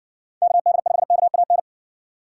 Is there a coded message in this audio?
Yes: Morse code from 0.4 to 1.6 s, heard as a keyed tone, about 60 wpm, 700 Hz. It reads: KD5KAN